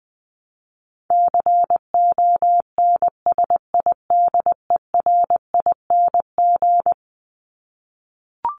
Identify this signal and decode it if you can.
Morse code, heard as a keyed tone, about 20 wpm, 700 Hz. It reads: CONSIDERING
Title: ING Suffix - Top 100 Words with Base Words 20wpm